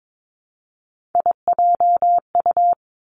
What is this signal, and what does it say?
Morse code, heard as a keyed tone, about 22 wpm, 700 Hz. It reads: IJU
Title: Three Challenging Letter-Number Combinations 22wpm